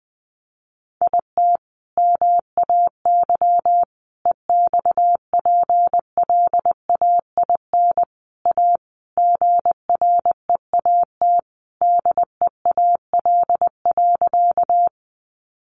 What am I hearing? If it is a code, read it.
Morse code, heard as a keyed tone, about 20 wpm, 700 Hz. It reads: IT MAY EXPLAIN A GREAT DEAL.
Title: Sentences from Top 1100 Words 20wpm